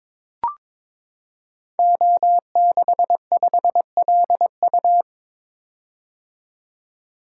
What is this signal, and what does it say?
Morse code, heard as a keyed tone, about 22 wpm, 700 Hz. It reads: O65LU
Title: Code Groups - 5 Letter:Numbers - 22wpm